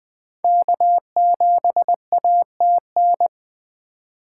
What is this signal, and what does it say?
Morse code, heard as a keyed tone, about 21 wpm, 700 Hz. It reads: K7ATN
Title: 2023 SOTA Chaser Call Signs - 20wpm